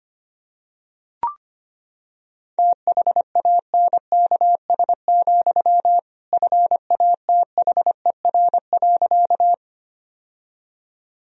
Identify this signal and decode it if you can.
Morse code, heard as a keyed tone, about 25 wpm, 700 Hz. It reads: THANKS, FATHER.
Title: Sentences from Top 1000 Words - Review Part 3 - 25wpm